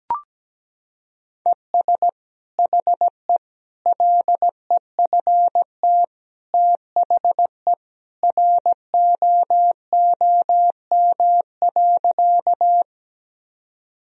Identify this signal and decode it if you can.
Morse code, heard as a keyed tone, about 17 wpm, 700 Hz. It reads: ES HE LEFT THE ROOM.
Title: Sentences from Top 300 Words and Abbreviations 17wpm